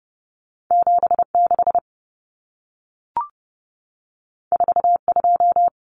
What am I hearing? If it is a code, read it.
Morse code, heard as a keyed tone, about 30 wpm, 700 Hz. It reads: 76 42
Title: Numbers - 2-Digit 30wpm